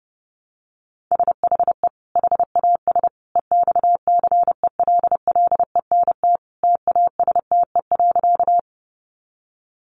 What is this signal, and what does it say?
Morse code, heard as a keyed tone, about 30 wpm, 700 Hz. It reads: SHE HAS EXCELLENT TASTE.
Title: Sentences from Top 1100 Words 30wpm